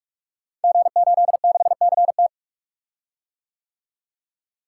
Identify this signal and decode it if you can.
Morse code, heard as a keyed tone, about 44 wpm, 700 Hz. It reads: G86CT
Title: Code Groups - 5 Letter:Numbers - 45wpm